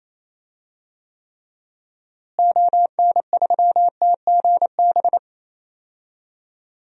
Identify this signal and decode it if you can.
Morse code, heard as a keyed tone, about 28 wpm, 700 Hz. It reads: ON3TGB
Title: DXCC-100 Call Signs - 6 Characters - Repeated 2x - 28wpm